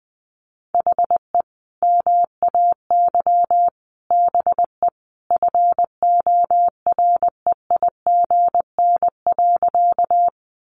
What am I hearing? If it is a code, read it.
Morse code, heard as a keyed tone, about 20 wpm, 700 Hz. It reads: HE MAY BE FOREIGN.
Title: Sentences from Top 1400 Words 20wpm